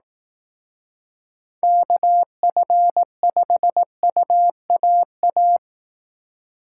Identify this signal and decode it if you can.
Morse code, heard as a keyed tone, about 18 wpm, 700 Hz. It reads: KF5UAA
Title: Speed Racing - USA 2x3 Call Signs 15wpm